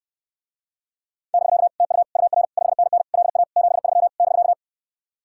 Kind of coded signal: Morse code

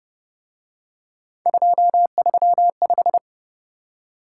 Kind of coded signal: Morse code

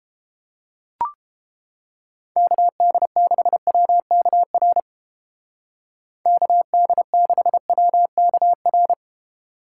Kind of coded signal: Morse code

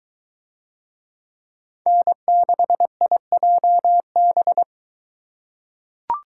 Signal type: Morse code

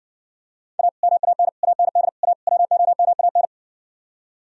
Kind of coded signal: Morse code

